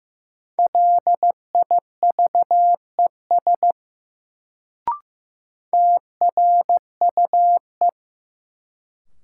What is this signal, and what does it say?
Morse code, heard as a keyed tone, about 15 wpm, 700 Hz. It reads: LIVES TRUE
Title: Sets of 5 Words Encore - New Word Drill - 15wpm